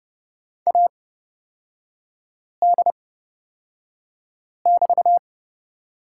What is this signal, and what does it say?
Morse code, heard as a keyed tone, about 29 wpm, spaced as for 5 wpm, 700 Hz. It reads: AD<BT>
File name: Lesson 37e - Letter x - Three Characters 30wpm